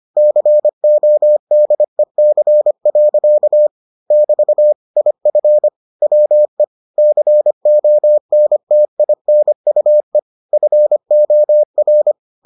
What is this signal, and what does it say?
Morse code, heard as a keyed tone, about 25 wpm, 600 Hz. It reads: CODEC. <BT> IF WE CONTINUE FOR